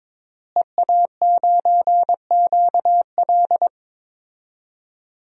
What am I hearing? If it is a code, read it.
Morse code, heard as a keyed tone, about 22 wpm, 700 Hz. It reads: EA9QL
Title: DXCC-100 Call Signs - 5 Characters - Repeated 3x - 22wpm